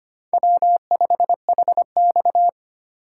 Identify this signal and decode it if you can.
Morse code, heard as a keyed tone, about 25 wpm, 700 Hz. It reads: W5HX
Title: USA Call Signs - 4 Characters - 3x Character Spacing - 25wpm